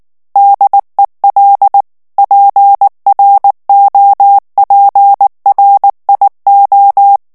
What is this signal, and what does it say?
Morse code, heard as a keyed tone, about 19 wpm, 800 Hz. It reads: DEL PROPRIO
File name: morse mp3 / ITALIEN texte clair 2 vitesse 18 mots minute